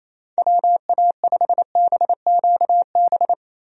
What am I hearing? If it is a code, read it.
Morse code, heard as a keyed tone, about 27 wpm, 700 Hz. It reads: WA5BQB
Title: USA Call Signs - 6 Characters - 28wpm